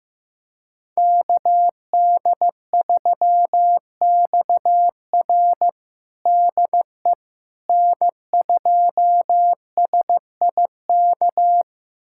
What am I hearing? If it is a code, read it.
Morse code, heard as a keyed tone, about 15 wpm, 700 Hz. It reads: KD3XR DE N2SIK
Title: USA Callsign DE Callsign - 5 Characters - 15wpm